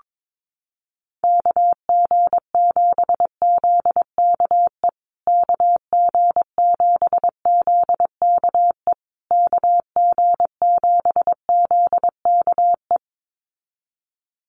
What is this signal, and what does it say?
Morse code, heard as a keyed tone, about 22 wpm, 700 Hz. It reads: KG7ZKE KG7ZKE KG7ZKE
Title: USA Call Signs Repeated 3x 22wpm